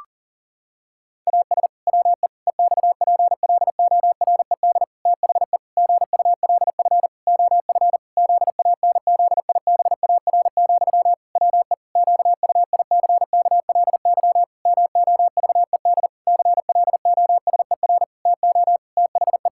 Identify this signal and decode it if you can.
Morse code, heard as a keyed tone, about 40 wpm, 700 Hz. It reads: AS WE EXPLORED THE GULF OF ZANZIBAR, WE QUICKLY MOVED CLOSER TO THE
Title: Pangram Challenge v2 - 40wpm